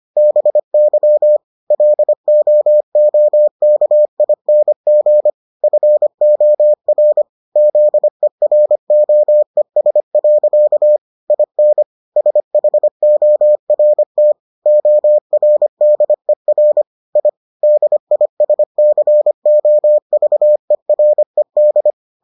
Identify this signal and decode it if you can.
Morse code, heard as a keyed tone, about 25 wpm, 600 Hz. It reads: BY LOOKING FOR ZEROES. IN SHORT ORDER I DISCOVERED